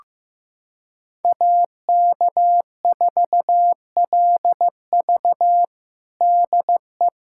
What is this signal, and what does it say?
Morse code, heard as a keyed tone, about 15 wpm, 700 Hz. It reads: AK4LV DE
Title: USA Callsign DE Callsign - 5 Characters - 15wpm